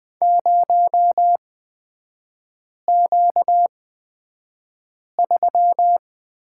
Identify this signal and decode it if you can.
Morse code, heard as a keyed tone, about 20 wpm, spaced as for 5 wpm, 690 Hz. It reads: 0Q3
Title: 236 Lesson 38e - Letter q - Three Characters 20wpm